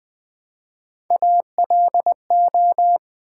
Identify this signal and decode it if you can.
Morse code, heard as a keyed tone, about 20 wpm, 700 Hz. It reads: ALO
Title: Build A Word - 1 Letter At A Time - 5-Letter Words - 20wpm